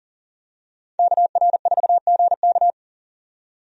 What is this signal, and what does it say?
Morse code, heard as a keyed tone, about 40 wpm, 700 Hz. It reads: KR4GK
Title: USA Call Signs - 5 Characters - 40wpm